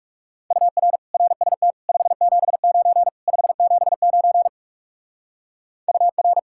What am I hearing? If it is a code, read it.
Morse code, heard as a keyed tone, about 36 wpm, 700 Hz. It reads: UR RST 579 579 UR
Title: Speed-Racing Common QSO Phrases - 30wpm